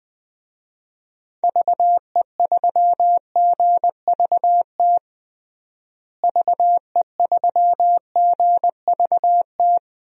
Morse code, heard as a keyed tone, about 21 wpm, 700 Hz. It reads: VE3GVT VE3GVT
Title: DXCC-100 Call Signs - 6 Characters - Repeated 3x - 20wpm